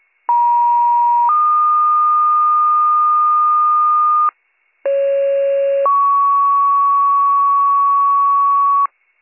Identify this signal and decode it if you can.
Two-tone page, heard as a one-second tone, then a three-second tone, about 950 then 1220 Hz, about 550 then 1120 Hz, with noise. It is heard from 0.3 to 8.9 s.